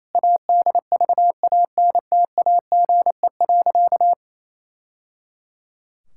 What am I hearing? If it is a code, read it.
Morse code, heard as a keyed tone, about 27 wpm, 700 Hz. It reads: ADVANTAGE.
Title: Sentences from Top 1000 Words - Review Part 6 - 28wpm